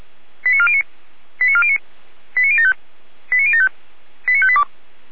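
Five-tone selective call (ZVEI), heard as quick runs of five tones, tones of about 70 ms. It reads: 89409 89409 89075 89075 89582
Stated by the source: ZVEI five-tone call received by SDR.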